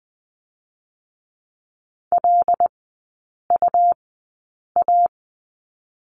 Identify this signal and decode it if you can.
Morse code, heard as a keyed tone, about 20 wpm, spaced as for 8 wpm, 700 Hz. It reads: LUA
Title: 127 Lesson 21e - Number 6 - Three Characters 20wpm